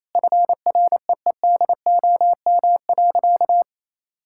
Morse code, heard as a keyed tone, about 28 wpm, 700 Hz. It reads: FREEDOM.
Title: Sentences from Top 1000 Words - Review Part 5 - 28wpm